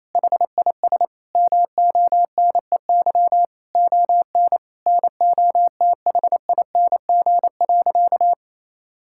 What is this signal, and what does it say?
Morse code, heard as a keyed tone, about 28 wpm, 700 Hz. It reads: HIS MONEY ON NOTHING.
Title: Sentences from Top 1000 Words - Review Part 5 - 28wpm